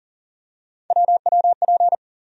Morse code, heard as a keyed tone, about 40 wpm, 700 Hz. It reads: WWP
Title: USA Call Sign Postfixes - 3 Letters 40wpm